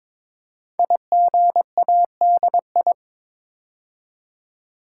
Morse code, heard as a keyed tone, about 22 wpm, 700 Hz. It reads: IGADI
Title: Code Groups - 5 Letters - 22wpm